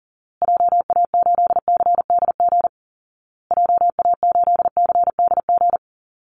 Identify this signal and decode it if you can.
Morse code, heard as a keyed tone, about 37 wpm, 700 Hz. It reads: JA8CDG JA8CDG
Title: Call Signs Repeated 2x - 40wpm